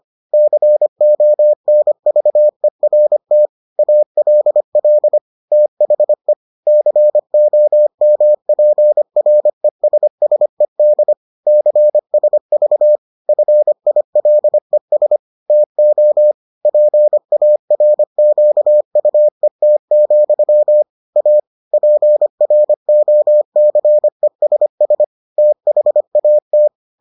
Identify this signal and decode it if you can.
Morse code, heard as a keyed tone, about 25 wpm, 600 Hz. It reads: CONVERT ALL THE COMPRESSED CSV FILES TO PARQUET, A PROCESS THAT